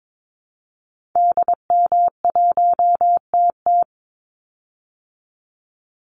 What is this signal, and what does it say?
Morse code, heard as a keyed tone, about 22 wpm, 700 Hz. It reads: DM1TT
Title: Call Signs Repeated 3x - 22wpm